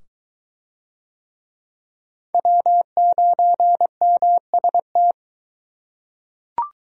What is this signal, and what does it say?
Morse code, heard as a keyed tone, about 22 wpm, 700 Hz. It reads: W9MST